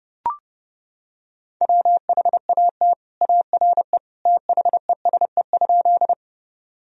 Morse code, heard as a keyed tone, about 30 wpm, 700 Hz. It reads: WHAT ARE THESE?
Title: Sentences from Top 1000 Words - Review Part 1 - 30wpm